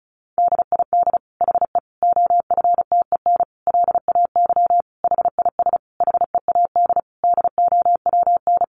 Morse code, heard as a keyed tone, about 34 wpm, 700 Hz. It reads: DID HE OFTEN LAY HIS HEAD DOWN
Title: Sentences from Top 1000 Words 35wpm